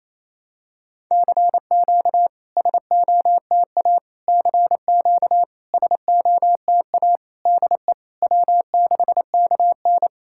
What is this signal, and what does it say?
Morse code, heard as a keyed tone, about 28 wpm, 690 Hz. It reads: CQ SOTA CQ SOTA DE W6KN
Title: SOTA QSO Practice 28wpm